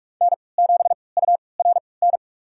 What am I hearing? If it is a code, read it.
Morse code, heard as a keyed tone, about 41 wpm, spaced as for 25 wpm, 700 Hz. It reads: N7URN
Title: USA Call Signs - 5 Characters - 3x Character Spacing - 45wpm